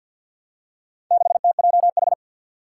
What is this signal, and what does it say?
Morse code, heard as a keyed tone, about 52 wpm, 700 Hz. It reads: 6TJH
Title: Four Challenging Letter-Number Combinations 50wpm